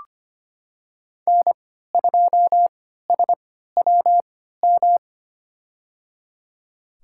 Morse code, heard as a keyed tone, about 25 wpm, spaced as for 14 wpm, 700 Hz. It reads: N2SWM
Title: USA Call Signs - 5 Characters - 3x Character Spacing - 25wpm